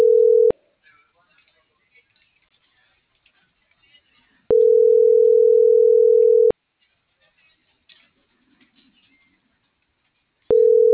Ringback tone (North America), 440 + 480 Hz.